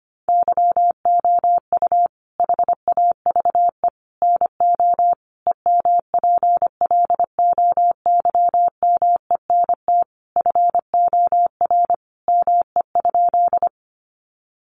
Morse code, heard as a keyed tone, about 25 wpm, 700 Hz. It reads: YOU HAVE NO EMPLOYMENT FOR ME?
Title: Sentences from Top 1500 Words 25wpm